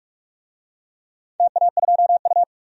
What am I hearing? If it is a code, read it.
Morse code, heard as a keyed tone, about 43 wpm, 700 Hz. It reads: TA2U
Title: DXCC-100 Call Signs - 4 Characters - 45wpm